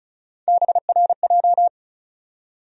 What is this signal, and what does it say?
Morse code, heard as a keyed tone, about 34 wpm, 700 Hz. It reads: BRJ
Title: USA Call Sign Postfixes - 3 Letters 35wpm